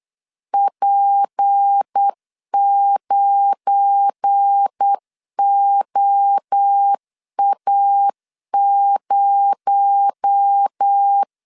Morse code, heard as a keyed tone, about 8 wpm, 800 Hz. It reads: P9OA0
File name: LEZIONI / 7 GF9Z / 7 GF9 8wpm 3545